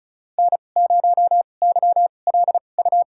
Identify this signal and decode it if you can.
Morse code, heard as a keyed tone, about 36 wpm, spaced as for 25 wpm, 700 Hz. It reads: N0YLU
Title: USA Call Signs - 5 Characters - 2x Character Spacing - 35wpm